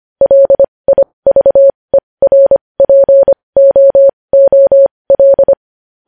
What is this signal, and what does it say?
Morse code, heard as a keyed tone, about 25 wpm, spaced as for 20 wpm, 550 Hz. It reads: LIVERPOOL
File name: podcast / News headlines-1x-25x20